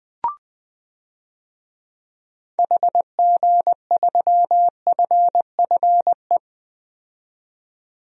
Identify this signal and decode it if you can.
Morse code, heard as a keyed tone, about 20 wpm, 700 Hz. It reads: HG3FFE
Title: DXCC-100 Call Signs - 6 Characters - 20wpm